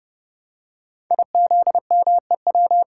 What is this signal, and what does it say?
Morse code, heard as a keyed tone, about 30 wpm, 700 Hz. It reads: IZMEW
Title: Code Groups - 5 Letters - 30wpm